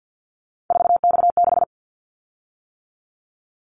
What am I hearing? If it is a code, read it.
Morse code, heard as a keyed tone, about 50 wpm, 700 Hz. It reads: 4X6
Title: Callsign Prefixes for Top 50 DXCC - 50wpm